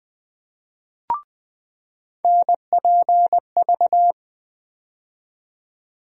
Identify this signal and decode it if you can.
Morse code, heard as a keyed tone, about 20 wpm, 700 Hz. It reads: NPV
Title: USA Call Sign Postfixes - 3 Letters 20wpm